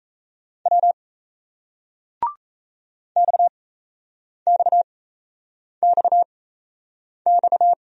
Morse code, heard as a keyed tone, about 39 wpm, spaced as for 8 wpm, 700 Hz. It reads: W XXXX